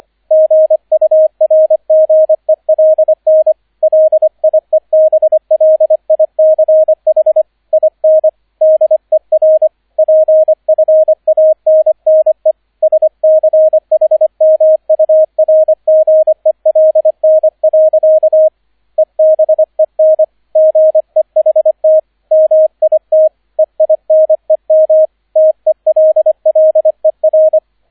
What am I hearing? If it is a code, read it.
Morse code, heard as a keyed tone, about 25 wpm, 630 Hz, with radio noise. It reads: GURGELN LIEBLICH IN DER PFANNE SCHMURGELN. EBEN GEHT MIT EINEM TELLER